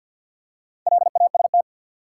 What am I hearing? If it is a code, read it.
Morse code, heard as a keyed tone, about 48 wpm, 700 Hz. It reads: LAST
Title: Four Letter Words - 50wpm